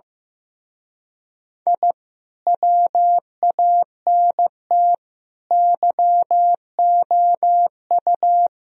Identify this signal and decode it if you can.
Morse code, heard as a keyed tone, about 17 wpm, 700 Hz. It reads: I WANT YOU
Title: Speed-Racing - Sets of 4 Words 15wpm